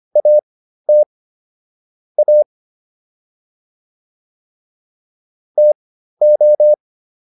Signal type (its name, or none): Morse code